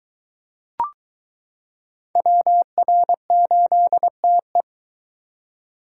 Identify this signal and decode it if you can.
Morse code, heard as a keyed tone, about 22 wpm, 700 Hz. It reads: WR8TE